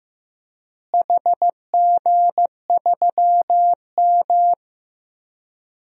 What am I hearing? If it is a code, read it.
Morse code, heard as a keyed tone, about 15 wpm, 700 Hz. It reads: HG3M
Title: DXCC-100 Call Signs - 4 Characters - Repeated 3x - 15wpm